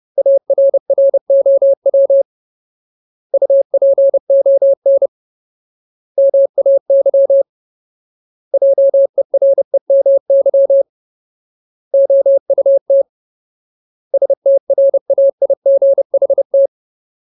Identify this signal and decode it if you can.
Morse code, heard as a keyed tone, about 30 wpm, 550 Hz. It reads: ARROW UPON MAY JEREMY OUT STRAIGHT
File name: Ultimate Word List / 30WPM Ultimate